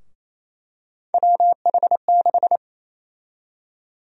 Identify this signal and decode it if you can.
Morse code, heard as a keyed tone, about 28 wpm, 700 Hz. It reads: WH6
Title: USA Call Sign Prefixes - 28wpm